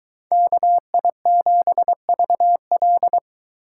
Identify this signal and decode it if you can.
Morse code, heard as a keyed tone, about 22 wpm, 700 Hz. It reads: KI7VL